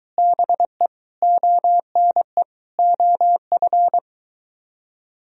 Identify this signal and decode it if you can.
Morse code, heard as a keyed tone, about 21 wpm, 700 Hz. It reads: BE ONE OF